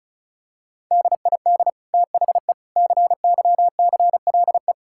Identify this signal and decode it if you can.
Morse code, heard as a keyed tone, about 34 wpm, 690 Hz. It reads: DID THE CYCLE
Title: Sentences from Top 1000 Words - Review Part 6 - 35wpm